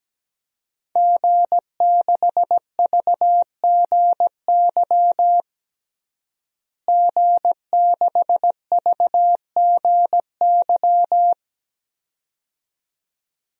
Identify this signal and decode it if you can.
Morse code, heard as a keyed tone, about 17 wpm, 700 Hz. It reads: G6VGY G6VGY
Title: DXCC-100 Call Signs - 5 Characters - Repeated 2x - 17wpm